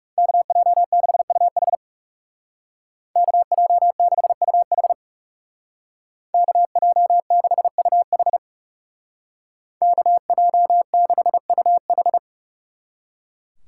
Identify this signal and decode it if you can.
Morse code, heard as a keyed tone, about 37 wpm, 700 Hz. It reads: KJ6UH KJ6UH KJ6UH KJ6UH
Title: Speed Racing - USA 2x2 Call Signs 30wpm